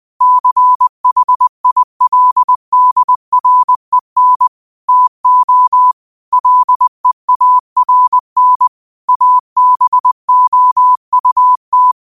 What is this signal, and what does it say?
Morse code, heard as a keyed tone, about 20 wpm, 1000 Hz. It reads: CHILDREN TO LEARN ABOUT
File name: qotdfiles / 20wpm